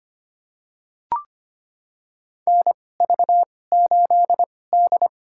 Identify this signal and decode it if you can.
Morse code, heard as a keyed tone, about 25 wpm, spaced as for 18 wpm, 700 Hz. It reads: NV8D
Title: USA Call Signs - 4 Characters - 2x Character Spacing - 25wpm